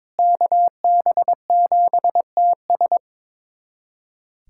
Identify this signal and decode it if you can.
Morse code, heard as a keyed tone, about 21 wpm, 700 Hz. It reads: KB7TS
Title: USA Callsign DE Callsign - 5 Characters - 22wpm